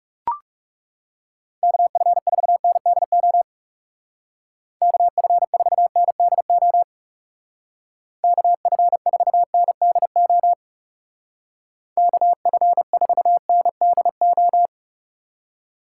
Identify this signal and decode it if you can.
Morse code, heard as a keyed tone, about 38 wpm, 700 Hz. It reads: KF4NDO KF4NDO KF4NDO KF4NDO
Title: Speed Racing - USA 2x3 Call Signs 30wpm